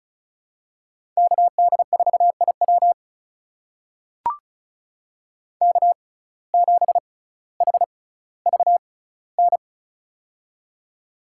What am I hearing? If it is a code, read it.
Morse code, heard as a keyed tone, about 34 wpm, spaced as for 18 wpm, 700 Hz. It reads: KD4IW K7HVN
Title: USA Call Signs - 5 Characters - 6x Character Spacing - 35wpm